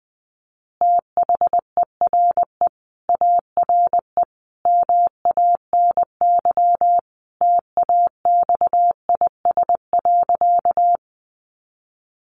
Morse code, heard as a keyed tone, about 20 wpm, 700 Hz. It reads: THERE ARE MANY TAXIS.
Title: Easy Phrases 20wpm